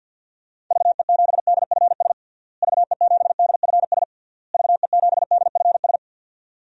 Morse code, heard as a keyed tone, about 52 wpm, 700 Hz. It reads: VE7DFS VE7DFS VE7DFS
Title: DXCC-100 Call Signs - 6 Characters - Repeated 3x - 50wpm